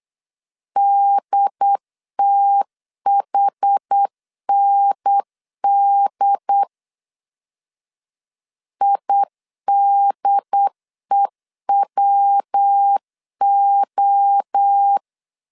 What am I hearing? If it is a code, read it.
Morse code, heard as a keyed tone, about 8 wpm, 800 Hz. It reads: DTHND IDEWO
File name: CW / DET / 1 3 1500C S B 8wpm 4141